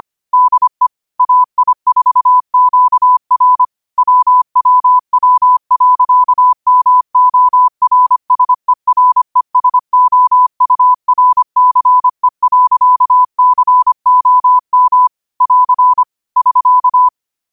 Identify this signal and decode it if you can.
Morse code, heard as a keyed tone, about 25 wpm, 1000 Hz. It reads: DE AI4QR WWW.MORSERESOURCE.COM <AR> <SK>
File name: qotdfiles / 25wpm